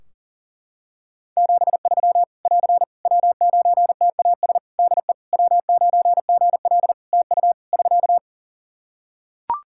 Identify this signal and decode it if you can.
Morse code, heard as a keyed tone, about 40 wpm, 700 Hz. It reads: 73 <AR> W9TAS DE W9GL TU <SK>
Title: Practice QSO - Exchange 4 - Ending - 40wpm